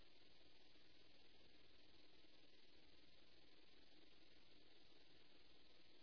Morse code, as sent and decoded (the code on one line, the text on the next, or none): none